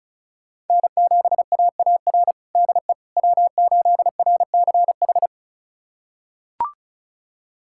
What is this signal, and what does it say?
Morse code, heard as a keyed tone, about 34 wpm, 700 Hz. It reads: N7AAR DE W8RCH
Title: USA Callsign DE Callsign - 5 Characters - 35wpm